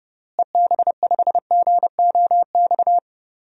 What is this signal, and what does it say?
Morse code, heard as a keyed tone, about 30 wpm, 700 Hz. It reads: EB5GOX
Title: DXCC-100 Call Signs - 6 Characters - Repeated 3x - 30wpm